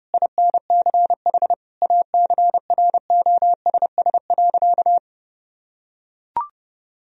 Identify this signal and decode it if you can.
Morse code, heard as a keyed tone, about 30 wpm, 700 Hz. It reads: INCH ACROSS.
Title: Sentences from Top 1000 Words - Review Part 3 - 30wpm